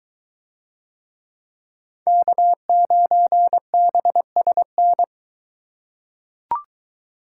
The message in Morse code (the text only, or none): K9BSN